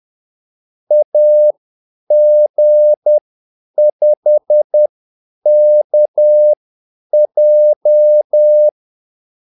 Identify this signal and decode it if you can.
Morse code, heard as a keyed tone, about 10 wpm, spaced as for 8 wpm, 600 Hz. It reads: AG5KJ
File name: Callsigns-10-8wpm